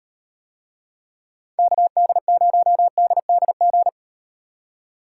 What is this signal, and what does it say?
Morse code, heard as a keyed tone, about 35 wpm, 700 Hz. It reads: KD0DDG